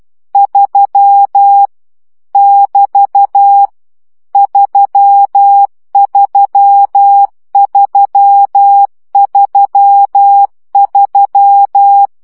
Morse code, heard as a keyed tone, about 12 wpm, 800 Hz. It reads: 3 <BT> 33333
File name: CW gyakorlas 3 10 baud